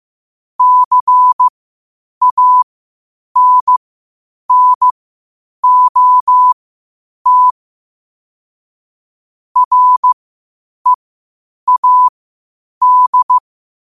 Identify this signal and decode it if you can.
Morse code, heard as a keyed tone, about 15 wpm, spaced as for 8 wpm, 1000 Hz. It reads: CANNOT READ
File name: 8wpm